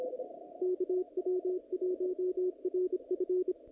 Morse code, heard as a keyed tone, about 25 wpm, 370 Hz, with radio noise. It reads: KW1RF
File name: contests / naqp-cw-k5tr-op-k5pi- -2100